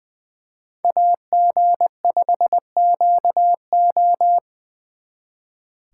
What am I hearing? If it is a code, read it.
Morse code, heard as a keyed tone, about 25 wpm, 700 Hz. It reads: AG5QO
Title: Speed Racing - USA Call Signs 20wpm